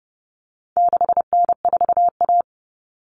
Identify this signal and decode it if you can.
Morse code, heard as a keyed tone, about 30 wpm, 700 Hz. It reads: 6N4A
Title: Common Cut Numbers - 4-Digit 30wpm